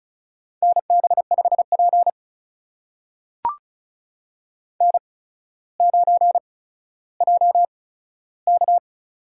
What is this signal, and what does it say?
Morse code, heard as a keyed tone, about 36 wpm, spaced as for 15 wpm, 700 Hz. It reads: NB5P N9JK
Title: USA Call Signs - 8x Character Spacing - 35wpm